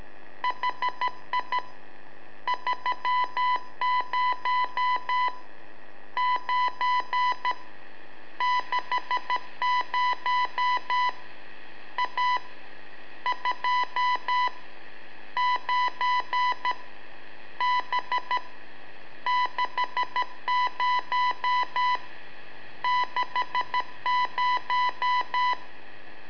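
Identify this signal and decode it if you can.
Morse code, heard as a keyed tone, about 13 wpm, 990 Hz. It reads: HI 30 9 60 A 2 9 B 60 60
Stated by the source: Telemetry in Morse from LU7AA and LU4EZ, AMSAT and Turma Mar del Plata, sent from a balloon.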